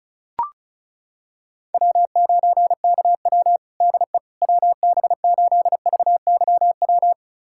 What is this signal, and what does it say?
Morse code, heard as a keyed tone, about 36 wpm, 700 Hz. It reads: W9KW DE WB8VYW
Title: USA Callsign DE Callsign 35wpm